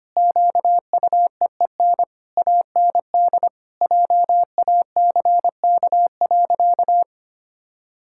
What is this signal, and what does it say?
Morse code, heard as a keyed tone, about 25 wpm, 700 Hz. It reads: QUEEN AND JACK.
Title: Pangram Challenge v2 - 25wpm